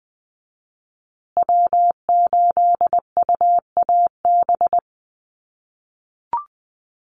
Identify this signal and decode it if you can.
Morse code, heard as a keyed tone, about 20 wpm, 700 Hz. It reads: W8UAB
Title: USA Call Signs Repeated 2x 20wpm